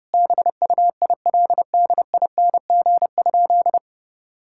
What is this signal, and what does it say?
Morse code, heard as a keyed tone, about 30 wpm, 700 Hz. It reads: BUILDING?
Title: Sentences from Top 1000 Words - Review Part 5 - 30wpm